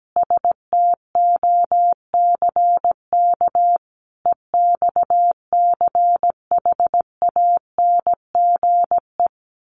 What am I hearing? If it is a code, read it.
Morse code, heard as a keyed tone, about 17 wpm, 700 Hz. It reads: STOCK EXCHANGE
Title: Sets of 5 Words 17wpm